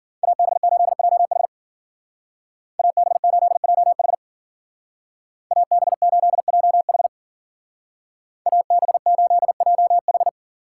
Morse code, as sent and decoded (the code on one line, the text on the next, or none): .- -... ---.. .--- .... / .- -... ---.. .--- .... / .- -... ---.. .--- .... / .- -... ---.. .--- ....
AB8JH AB8JH AB8JH AB8JH